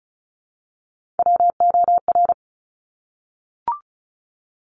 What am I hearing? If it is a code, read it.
Morse code, heard as a keyed tone, about 36 wpm, 700 Hz. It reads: WOR
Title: Top Three Letter Combinations in Words 35wpm